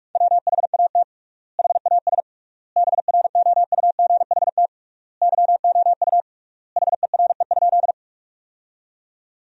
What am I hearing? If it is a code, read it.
Morse code, heard as a keyed tone, about 42 wpm, 700 Hz. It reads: WHAT HAS BROUGHT YOU HERE?
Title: 3x Word-Spacing - Sentences from Top 500 Words - 45wpm